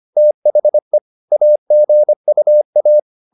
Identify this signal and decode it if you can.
Morse code, heard as a keyed tone, about 25 wpm, 600 Hz. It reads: THE AGUA
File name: the old man and the sea part 1 25wpm